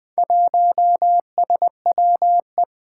Morse code, heard as a keyed tone, about 20 wpm, 700 Hz. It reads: 1SWE
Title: Four Challenging Letter-Number Combinations 20wpm